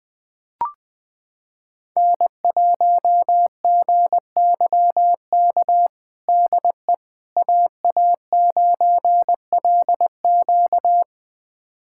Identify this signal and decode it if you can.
Morse code, heard as a keyed tone, about 20 wpm, 700 Hz. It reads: N1GYK DE AA9LQ
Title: USA Callsign DE Callsign - 5 Characters - 20wpm